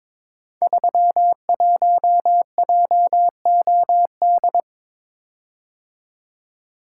Morse code, heard as a keyed tone, about 23 wpm, 700 Hz. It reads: 31JOD
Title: Code Groups - 5 Letter:Numbers - 22wpm